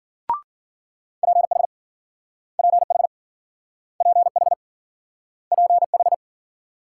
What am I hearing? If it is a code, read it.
Morse code, heard as a keyed tone, about 53 wpm, 700 Hz. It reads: PH PH PH PH